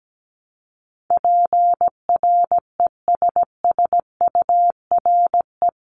Morse code, heard as a keyed tone, about 17 wpm, 700 Hz. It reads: PRESSURE
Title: Multi-letter Phonemes Repeated 3x 17wpm